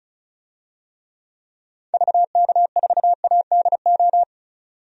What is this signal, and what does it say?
Morse code, heard as a keyed tone, about 36 wpm, 700 Hz. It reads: VK4ADO
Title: DXCC-100 Call Signs - 6 Characters - 35wpm